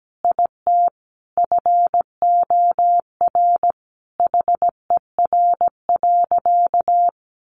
Morse code, heard as a keyed tone, about 17 wpm, 700 Hz. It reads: IT FOR HER.
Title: Sentences from Top 1500 Words 17wpm